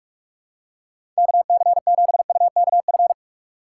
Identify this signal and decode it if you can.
Morse code, heard as a keyed tone, about 42 wpm, 700 Hz. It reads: KC7UKF
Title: USA Call Signs - 6 Characters - 2x Character Spacing - 45wpm